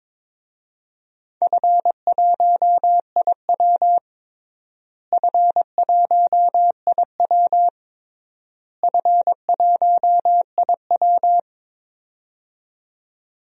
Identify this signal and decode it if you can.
Morse code, heard as a keyed tone, about 22 wpm, 700 Hz. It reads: F1IW F1IW F1IW
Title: DXCC-100 Call Signs - 4 Characters - Repeated 3x - 22wpm